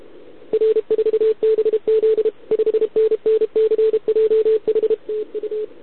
Morse code, heard as a keyed tone, about 32 wpm, 440 Hz. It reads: R4BZ 5NN CJH TU
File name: ARCHIV / OKDX2020 / OK1RIb